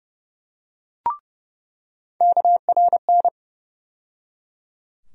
Morse code, heard as a keyed tone, about 30 wpm, 700 Hz. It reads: KRN
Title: USA Call Sign Postfixes - 3 Letters 30wpm